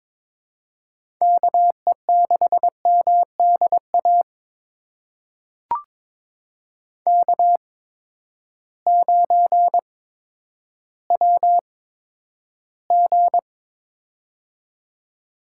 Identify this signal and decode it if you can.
Morse code, heard as a keyed tone, about 22 wpm, spaced as for 11 wpm, 700 Hz. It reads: KE6MDA K9WG
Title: USA Call Signs - 8x Character Spacing - 22wpm